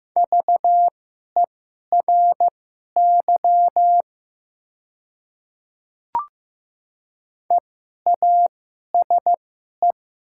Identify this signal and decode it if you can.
Morse code, heard as a keyed tone, about 15 wpm, spaced as for 11 wpm, 700 Hz. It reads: VERY EASE
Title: 2x Character Spacing - Top 500 Words - 15wpm